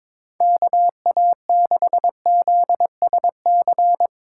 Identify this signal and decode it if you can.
Morse code, heard as a keyed tone, about 22 wpm, 700 Hz. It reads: KA6ZSC
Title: USA Call Signs - 6 Characters - 2x Character Spacing - 22wpm